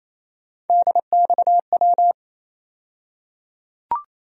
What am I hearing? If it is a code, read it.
Morse code, heard as a keyed tone, about 28 wpm, 700 Hz. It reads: DXW
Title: USA Call Sign Postfixes - 3 Letters 28wpm